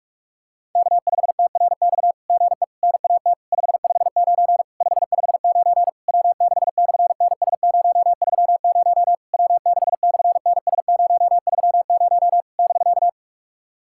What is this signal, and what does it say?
Morse code, heard as a keyed tone, about 43 wpm, 690 Hz. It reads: K5TRX GE NAT 559 559 W6/NS030 W6/NS030 <BK>
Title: SOTA QSO Practice 45wpm